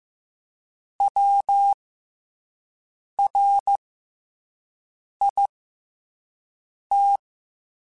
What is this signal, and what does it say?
Morse code, heard as a keyed tone, about 15 wpm, 800 Hz. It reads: W R I T
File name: Special INGs-mp3s / Special INGs-mp3s / SlowToFaster ing @10